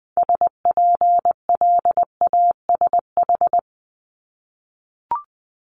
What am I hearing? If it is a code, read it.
Morse code, heard as a keyed tone, about 20 wpm, 700 Hz. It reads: SPLASH
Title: Multi-letter Phonemes Repeated 3x 20wpm